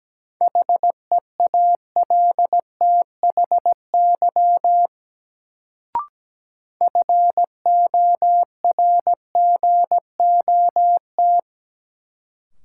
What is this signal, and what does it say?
Morse code, heard as a keyed tone, about 17 wpm, 700 Hz. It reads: HEALTHY FORGOT
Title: Sentences from Top 1400 Words - New Word Drill Alt - 17wpm